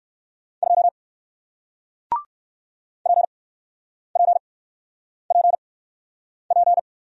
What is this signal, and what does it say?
Morse code, heard as a keyed tone, about 61 wpm, spaced as for 9 wpm, 700 Hz. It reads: 2 PPPP